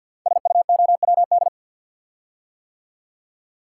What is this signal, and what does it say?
Morse code, heard as a keyed tone, about 50 wpm, 700 Hz. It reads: SUOWD
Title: Code Groups - 5 Letters - 50wpm